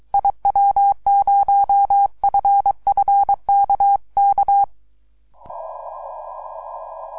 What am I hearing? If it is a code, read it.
Morse code, heard as a keyed tone, about 23 wpm, 800 Hz. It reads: IW0FFK K
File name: iw0ffk / 2012 04 02 3B8DB IW0FFK